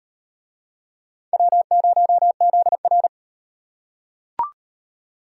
Morse code, heard as a keyed tone, about 35 wpm, 700 Hz. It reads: W0ZR